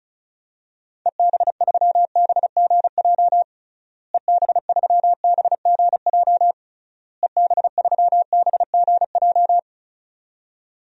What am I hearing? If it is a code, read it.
Morse code, heard as a keyed tone, about 34 wpm, 700 Hz. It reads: EB3BGJ EB3BGJ EB3BGJ
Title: DXCC-100 Call Signs - 6 Characters - Repeated 3x - 35wpm